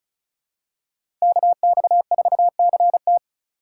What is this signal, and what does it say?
Morse code, heard as a keyed tone, about 36 wpm, 700 Hz. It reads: KX4CT
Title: USA Call Signs - 5 Characters - 4x Character Spacing - 35wpm